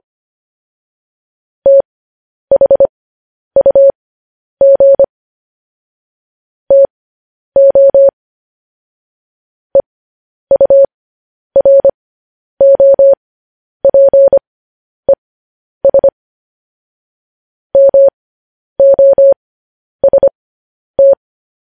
Morse code, heard as a keyed tone, about 25 wpm, spaced as for 10 wpm, 550 Hz. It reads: THUG TO EUROPES MOST